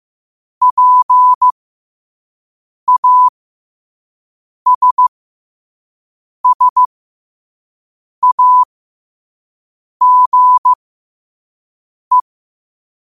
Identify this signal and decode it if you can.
Morse code, heard as a keyed tone, about 15 wpm, spaced as for 5 wpm, 1000 Hz. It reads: PASSAGE
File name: qotdfiles / 5wpm